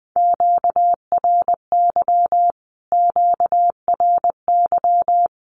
Morse code, heard as a keyed tone, about 20 wpm, 700 Hz. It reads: QRY QRY
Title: Q-Codes Repeated 3x 20wpm